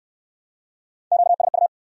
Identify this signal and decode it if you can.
Morse code, heard as a keyed tone, about 66 wpm, 700 Hz. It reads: QSU